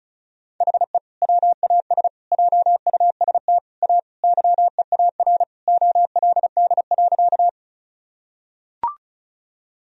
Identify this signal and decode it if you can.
Morse code, heard as a keyed tone, about 36 wpm, 700 Hz. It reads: HE WAS JUST A YEAR OLD.
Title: Sentences from Top 1000 Words - Review Part 1 - 35wpm